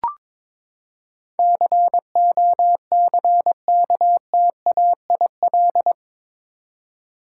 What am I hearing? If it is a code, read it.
Morse code, heard as a keyed tone, about 21 wpm, 700 Hz. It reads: COCKTAIL
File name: practice / 22wpm / Eight Letter Words - 22wpm